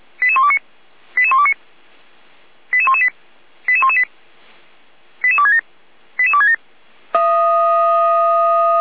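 Five-tone selective call (ZVEI), heard as quick runs of five tones, tones of about 70 ms. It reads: 88129 88129 88118 88118 88267 88267
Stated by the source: ZVEI five-tone call 88235 received by SDR, archived by feuerwehr-orlamuende.org fire brigade.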